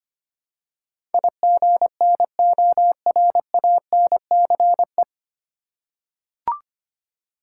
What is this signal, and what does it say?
Morse code, heard as a keyed tone, about 25 wpm, 690 Hz. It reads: IGNORANCE
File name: practice / 25wpm / 9 Letter Words - First 3 Letters Given - 25wpm